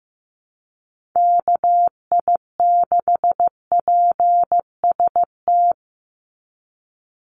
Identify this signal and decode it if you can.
Morse code, heard as a keyed tone, about 15 wpm, 700 Hz. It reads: KI6PST
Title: USA Call Signs 15wpm